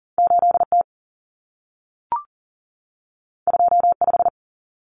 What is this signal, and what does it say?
Morse code, heard as a keyed tone, about 40 wpm, 700 Hz. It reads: 8T 25
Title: Common Cut Numbers - 2-Digit 40wpm